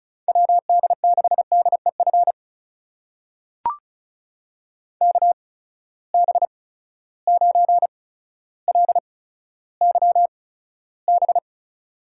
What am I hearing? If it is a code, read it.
Morse code, heard as a keyed tone, about 36 wpm, spaced as for 15 wpm, 700 Hz. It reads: WD6DEF KB9LYB
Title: USA Call Signs - 6 Characters - 8x Character Spacing - 35wpm